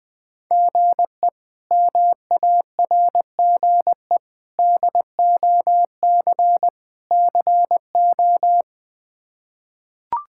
Morse code, heard as a keyed tone, about 20 wpm, 700 Hz. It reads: GE MARGE DOC CO
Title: K1USN SST QSO Practice - 20wpm